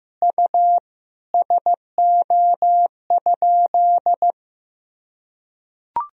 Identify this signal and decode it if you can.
Morse code, heard as a keyed tone, about 14 wpm, 700 Hz. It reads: U SO?
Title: Sentences from Top 400 Words and Abbreviations 15wpm